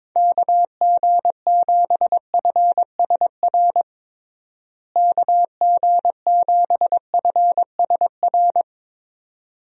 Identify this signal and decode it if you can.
Morse code, heard as a keyed tone, about 22 wpm, 700 Hz. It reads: KG7FSR KG7FSR
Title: USA Call Signs - 6 Characters - Repeated 2x - 22wpm